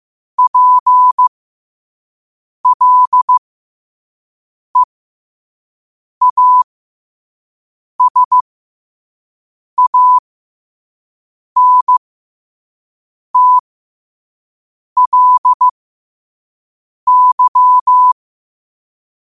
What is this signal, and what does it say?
Morse code, heard as a keyed tone, about 15 wpm, spaced as for 5 wpm, 1000 Hz. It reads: PLEASANTLY
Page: New quotes every day in morse code at 5 Words per minute.